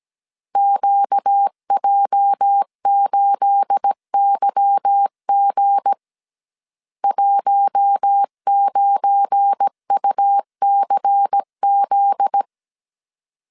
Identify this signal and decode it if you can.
Morse code, heard as a keyed tone, about 17 wpm, 800 Hz. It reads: QJ8YG 19UCZ
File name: CW / DET / 1 9 1500C S A 16wpm 2828